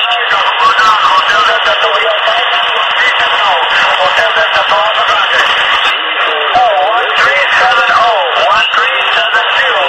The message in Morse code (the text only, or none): none